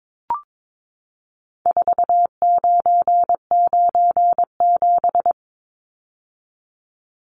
Morse code, heard as a keyed tone, about 22 wpm, 700 Hz. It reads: 4997
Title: Numbers - 4-Digit 22wpm